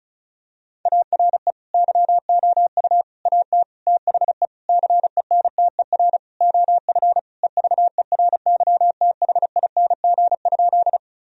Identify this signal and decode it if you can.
Morse code, heard as a keyed tone, about 36 wpm, 700 Hz. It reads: ARE YOU AT THE CENTER OF EVERYTHING?
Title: Sentences from Top 1000 Words - Review Part 2 - 35wpm